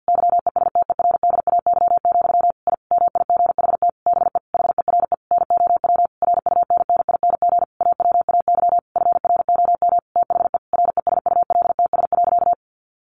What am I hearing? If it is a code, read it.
Morse code, heard as a keyed tone, about 50 wpm, 700 Hz. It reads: YESTERDAY, I MIGHT BE HERE NOW RUNNING AWAY FROM THE RESULTS.
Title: A Sound Of Thunder 50wpm